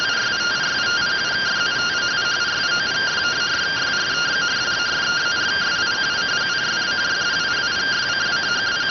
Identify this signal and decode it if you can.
Two-tone teleprinter signal, about 170 Hz shift.